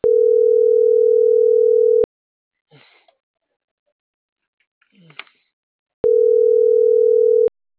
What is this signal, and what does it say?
Ringback tone (North America), 440 + 480 Hz.